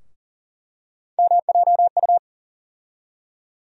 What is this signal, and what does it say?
Morse code, heard as a keyed tone, about 40 wpm, 700 Hz. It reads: MJU
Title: USA Call Sign Postfixes - 3 Letters 40wpm